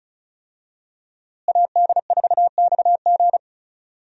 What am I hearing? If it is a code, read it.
Morse code, heard as a keyed tone, about 34 wpm, 700 Hz. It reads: AD4XG
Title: USA Call Signs - 5 Characters - Repeated 2x - 35wpm